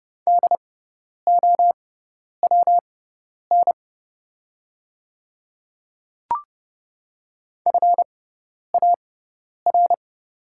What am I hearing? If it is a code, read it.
Morse code, heard as a keyed tone, about 30 wpm, spaced as for 10 wpm, 700 Hz. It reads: DOWN FAR
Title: 6x Character Spacing - Top 500 Words - 30wpm